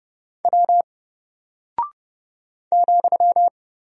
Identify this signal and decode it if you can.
Morse code, heard as a keyed tone, about 30 wpm, 700 Hz. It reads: W ,
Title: Single Character (Letter:Number:Punctuation:Prosign) Alt 30wpm